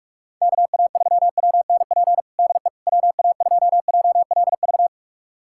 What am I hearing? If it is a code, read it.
Morse code, heard as a keyed tone, about 46 wpm, 700 Hz. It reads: KA3WNP DE WA2JLV
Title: USA Callsign DE Callsign - 6 Characters - 45wpm